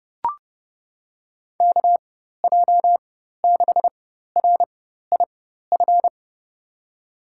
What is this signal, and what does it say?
Morse code, heard as a keyed tone, about 29 wpm, spaced as for 14 wpm, 700 Hz. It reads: KJ6RIF
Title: USA Call Signs - 6 Characters - 4x Character Spacing - 30wpm